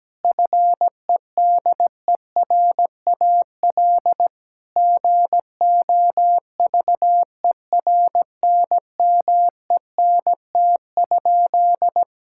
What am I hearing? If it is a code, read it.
Morse code, heard as a keyed tone, about 17 wpm, 700 Hz. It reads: FEDERAL GOVERNMENT?
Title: Sentences from Top 1000 Words - Review Part 5 - 17wpm